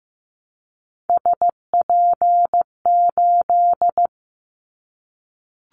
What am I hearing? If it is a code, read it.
Morse code, heard as a keyed tone, about 15 wpm, 700 Hz. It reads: SP8
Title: Callsign Prefixes for Top 25 DXCC - 15wpm